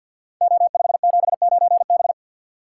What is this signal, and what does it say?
Morse code, heard as a keyed tone, about 51 wpm, 700 Hz. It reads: O579B
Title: Code Groups - 5 Letter:Numbers - 50wpm